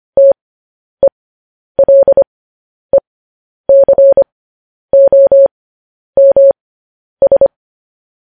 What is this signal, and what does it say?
Morse code, heard as a keyed tone, about 25 wpm, spaced as for 10 wpm, 550 Hz. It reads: TELECOMS
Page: News Headlines in Morse Code at 10 WPM